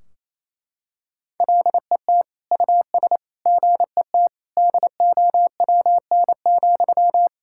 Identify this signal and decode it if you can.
Morse code, heard as a keyed tone, about 28 wpm, 700 Hz. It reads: LET US GET DOWN,
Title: Sentences from Top 1600 Words 28wpm